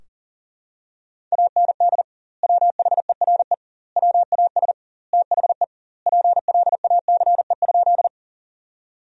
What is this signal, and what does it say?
Morse code, heard as a keyed tone, about 40 wpm, 700 Hz. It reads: AND WHERE WAS THE PLACE?
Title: 2x Word-Spacing - Sentences from Top 200 Words - 40wpm